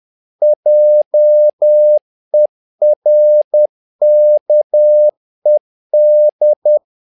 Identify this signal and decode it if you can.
Morse code, heard as a keyed tone, about 10 wpm, 600 Hz. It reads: JERKED
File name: learning center / tower trouble / tower trouble 10wpm